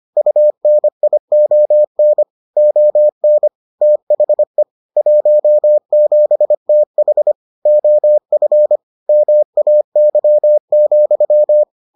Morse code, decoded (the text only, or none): UNION ON THE 17TH OF MAY,